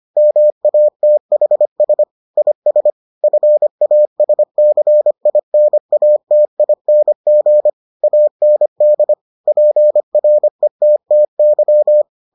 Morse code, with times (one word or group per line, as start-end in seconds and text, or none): 0.2-2.0 MATHS
2.4-2.9 IS
3.2-7.7 FASCINATING
8.0-9.1 AND
9.5-12.0 PRETTY